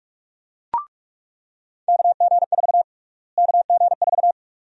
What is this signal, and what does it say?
Morse code, heard as a keyed tone, about 46 wpm, 700 Hz. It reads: KG4 KG4
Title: USA Call Sign Prefixes - Repeated 2x - 45wpm